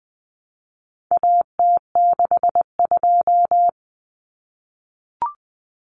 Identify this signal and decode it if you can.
Morse code, heard as a keyed tone, about 20 wpm, 700 Hz. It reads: AT62
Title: Common Cut Numbers - 4-Digit 20wpm